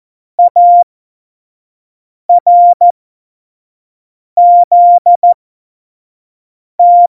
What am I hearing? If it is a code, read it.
Morse code, heard as a keyed tone, about 14 wpm, spaced as for 5 wpm, 700 Hz. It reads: ARZT
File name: Morsekurs / Witze / Witz 12. Dezember 2017, 25 F70